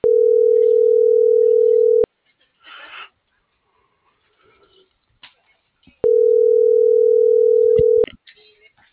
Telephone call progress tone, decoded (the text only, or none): ringback tone